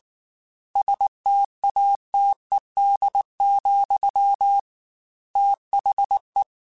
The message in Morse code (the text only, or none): STATED, THE